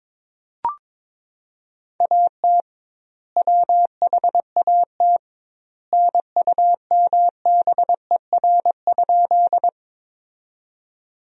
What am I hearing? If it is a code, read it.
Morse code, heard as a keyed tone, about 22 wpm, 700 Hz. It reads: AT WHAT NUMBER?
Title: 2x Word-Spacing - Sentences from Top 100 Words - 22wpm